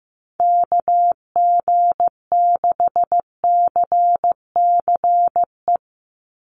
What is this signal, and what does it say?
Morse code, heard as a keyed tone, about 15 wpm, 700 Hz. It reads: KG6CCE
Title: USA Call Signs 15wpm